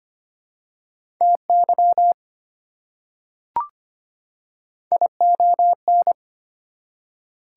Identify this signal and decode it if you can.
Morse code, heard as a keyed tone, about 25 wpm, 700 Hz. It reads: TY ION
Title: Prefixes and Suffixes 25wpm